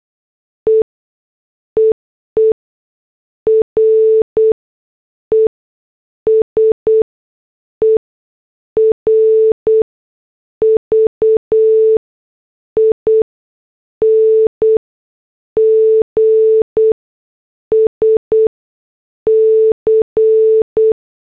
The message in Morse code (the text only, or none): E EE RESERVINGS C